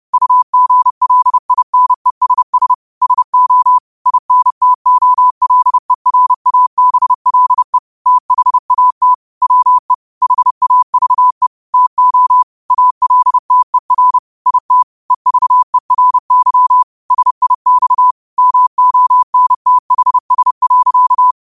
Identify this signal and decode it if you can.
Morse code, heard as a keyed tone, about 30 wpm, 1000 Hz. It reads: UGLINESS SO INTOLERABLE THAT WE HAVE TO ALTER IT EVERY SIX MONTHS.
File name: qotdfiles / 30wpm